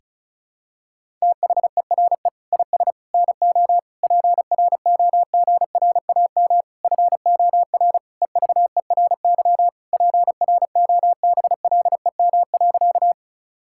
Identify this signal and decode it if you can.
Morse code, heard as a keyed tone, about 36 wpm, 700 Hz. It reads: THERE IS NO PROGRAM FOR EVERY PROBLEM.
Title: Sentences from Top 1000 Words - Review Part 5 - 35wpm